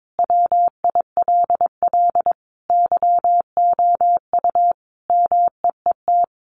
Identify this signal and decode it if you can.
Morse code, heard as a keyed tone, about 22 wpm, 700 Hz. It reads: WILL YOU MEET
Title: Sentences from Top 1100 Words 22wpm